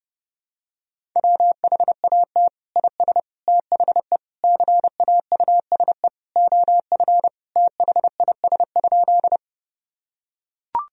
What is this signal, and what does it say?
Morse code, heard as a keyed tone, about 29 wpm, 700 Hz. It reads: WHAT IS THE CAUSE OF THIS?
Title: Sentences from Top 1000 Words - Review Part 1 - 30wpm